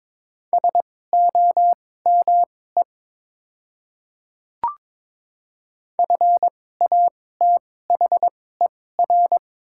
Morse code, heard as a keyed tone, about 22 wpm, spaced as for 16 wpm, 700 Hz. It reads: SOME FATHER
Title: 2x Character Spacing - Top 200 Words - 22wpm